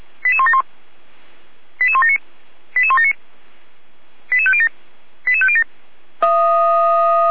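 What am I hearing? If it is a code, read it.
Five-tone selective call (ZVEI), heard as quick runs of five tones, tones of about 70 ms. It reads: 88171 88179 88179 88507 88507
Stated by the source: ZVEI five-tone call received on SDR.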